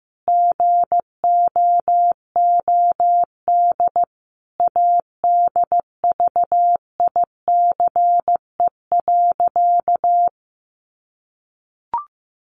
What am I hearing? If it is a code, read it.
Morse code, heard as a keyed tone, about 15 wpm, 700 Hz. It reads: GOOD ADVICE.
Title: Sentences from Top 1100 Words 15wpm